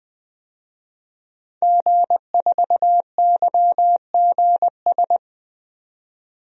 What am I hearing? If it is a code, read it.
Morse code, heard as a keyed tone, about 20 wpm, 700 Hz. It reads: G4YGS
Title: DXCC-100 Call Signs - 5 Characters - 20wpm